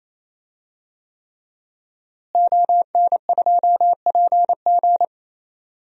Morse code, heard as a keyed tone, about 28 wpm, 700 Hz. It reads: ON2PG
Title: DXCC-100 Call Signs - 5 Characters - Repeated 3x - 28wpm